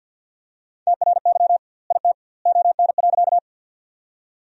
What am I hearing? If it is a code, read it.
Morse code, heard as a keyed tone, about 52 wpm, 700 Hz. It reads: TRY IT ON.
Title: 2x Word-Spacing - Sentences from Top 200 Words - 50wpm